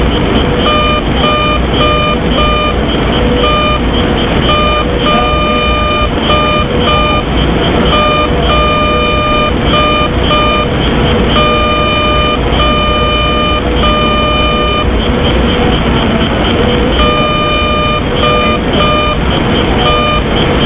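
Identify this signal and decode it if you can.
Morse code, heard as a keyed tone, about 4 wpm, 1210 Hz. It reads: HELLO DE
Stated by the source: Morse generated as RF by an IBM 1401 computer's memory cycles, heard near 630kHz at the bottom of the AM broadcast band, Mountain View California.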